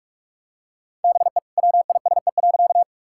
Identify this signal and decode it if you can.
Morse code, heard as a keyed tone, about 46 wpm, 700 Hz. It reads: BE WISE.
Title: Sentences from Top 1600 Words 45wpm